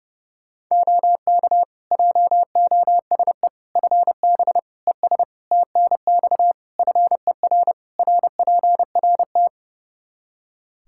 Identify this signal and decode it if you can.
Morse code, heard as a keyed tone, about 30 wpm, 700 Hz. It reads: OK JOSE FB ES TNX FER RPRT
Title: K1USN SST QSO Practice 30wpm